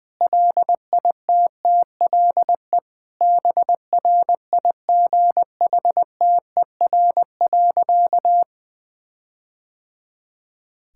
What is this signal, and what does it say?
Morse code, heard as a keyed tone, about 20 wpm, 700 Hz. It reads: LITTLE BRIGHTER.
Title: All-Age Appropriate Jokes - 20wpm